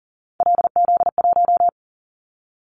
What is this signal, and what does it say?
Morse code, heard as a keyed tone, about 40 wpm, 700 Hz. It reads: LZ1
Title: Callsign Prefixes for Top 50 DXCC - 40wpm